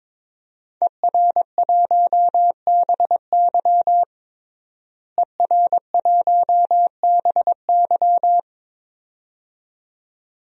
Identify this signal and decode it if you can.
Morse code, heard as a keyed tone, about 22 wpm, 700 Hz. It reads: ER1BY ER1BY
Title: DXCC-100 Call Signs - 5 Characters - Repeated 2x - 22wpm